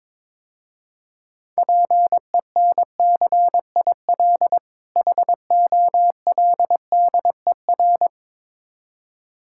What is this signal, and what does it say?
Morse code, heard as a keyed tone, about 23 wpm, 700 Hz. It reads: PENCIL HOLDER
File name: Easy Words - Speed-Racing with Hints - 22wpm